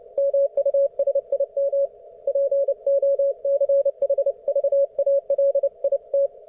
Morse code, heard as a keyed tone, about 29 wpm, 560 Hz. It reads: MUSIM POCHVALIT